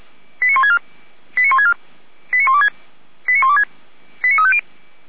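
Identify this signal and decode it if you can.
Five-tone selective call (ZVEI), heard as quick runs of five tones, tones of about 70 ms. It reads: 89164 89164 89126 89126 89350
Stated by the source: Five-tone ZVEI selective call received on an SDR.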